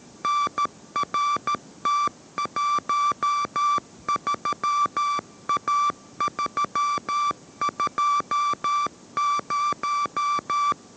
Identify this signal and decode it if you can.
Morse code, heard as a keyed tone, about 13 wpm, 1160 Hz. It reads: NRT13A320